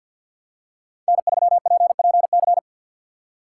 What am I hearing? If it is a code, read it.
Morse code, heard as a keyed tone, about 53 wpm, 700 Hz. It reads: N3PPC
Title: USA Call Signs - 5 Characters - 2x Character Spacing - 50wpm